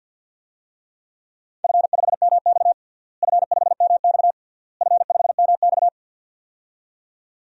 Morse code, heard as a keyed tone, about 49 wpm, 700 Hz. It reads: F5MX F5MX F5MX
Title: DXCC-100 Call Signs - 4 Characters - Repeated 3x - 50wpm